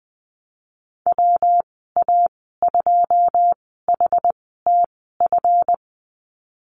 Morse code, heard as a keyed tone, about 20 wpm, spaced as for 15 wpm, 700 Hz. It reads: WA2HTF
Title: 173 Lesson 28g - Number 9 - Call Signs 20wpm